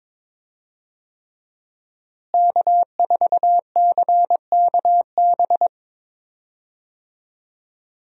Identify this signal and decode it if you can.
Morse code, heard as a keyed tone, about 22 wpm, 700 Hz. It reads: K4CKB
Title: USA Call Signs - 5 Characters - Repeated 3x - 22wpm